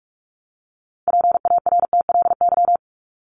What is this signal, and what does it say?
Morse code, heard as a keyed tone, about 44 wpm, 700 Hz. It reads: PARTLY
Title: LY Suffix - Top 100 Words with Base Words 45wpm